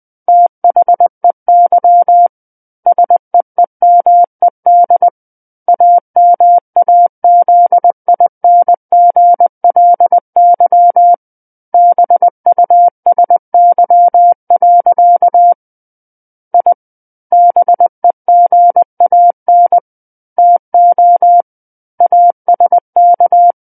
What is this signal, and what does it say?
Morse code, heard as a keyed tone, about 20 wpm, 700 Hz. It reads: THEY SEEMED AMAZINGLY BUSY. I BEGAN TO ASK
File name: 20 WPM